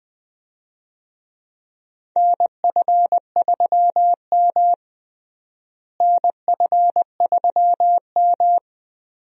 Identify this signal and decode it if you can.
Morse code, heard as a keyed tone, about 20 wpm, 700 Hz. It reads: NF3M NF3M
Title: USA Call Signs - 4 Characters - Repeated 2x - 20wpm